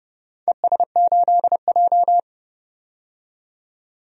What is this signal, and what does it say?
Morse code, heard as a keyed tone, about 30 wpm, 700 Hz. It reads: ES8J
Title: DXCC-100 Call Signs - 4 Characters - 30wpm